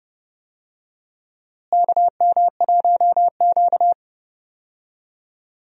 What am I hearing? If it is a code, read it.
Morse code, heard as a keyed tone, about 30 wpm, 700 Hz. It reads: KM1Q
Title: USA Call Signs - 4 Characters - Repeated 3x - 30wpm